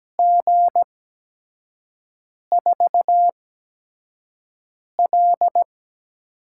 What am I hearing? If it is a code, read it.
Morse code, heard as a keyed tone, about 17 wpm, spaced as for 5 wpm, 700 Hz. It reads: G4L
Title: USA Call Signs - 8x Character Spacing - 17wpm